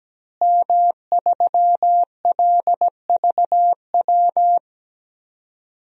Morse code, heard as a keyed tone, about 17 wpm, 700 Hz. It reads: M3LVW
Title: DXCC-100 Call Signs - 5 Characters - Repeated 3x - 17wpm